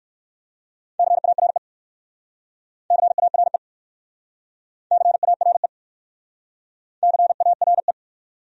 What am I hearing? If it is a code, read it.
Morse code, heard as a keyed tone, about 58 wpm, 690 Hz. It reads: CARE CARE CARE CARE